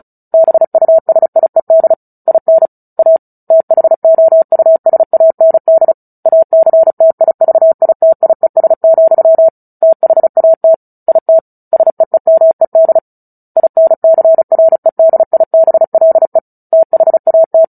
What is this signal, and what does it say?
Morse code, heard as a keyed tone, about 34 wpm, 650 Hz. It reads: BUSIED IN A THOUSAND ACTIVITIES, THAT IT SEEMED INCREDIBLE THAT
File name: Book Two / 09-Chapter 9 - 35 WPM